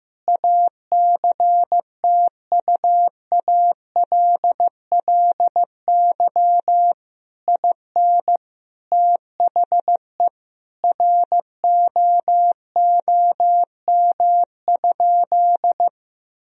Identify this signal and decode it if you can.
Morse code, heard as a keyed tone, about 15 wpm, 700 Hz. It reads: ACTUALLY IN THE ROOM?
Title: Sentences from Top 1000 Words - Review Part 4 - 15wpm